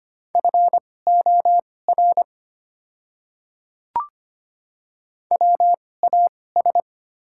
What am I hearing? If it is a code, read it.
Morse code, heard as a keyed tone, about 25 wpm, spaced as for 18 wpm, 690 Hz. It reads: FOR WAS
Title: 2x Character Spacing - Top 100 Words - 25wpm